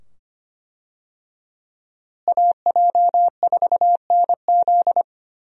Morse code, heard as a keyed tone, about 25 wpm, 700 Hz. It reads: AJ4NZ
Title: USA Call Signs - 5 Characters - Repeated 2x - 25wpm